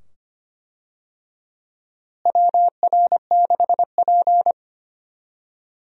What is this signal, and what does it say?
Morse code, heard as a keyed tone, about 25 wpm, 700 Hz. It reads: WR6P
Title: USA Call Signs - 4 Characters - Repeated 2x - 25wpm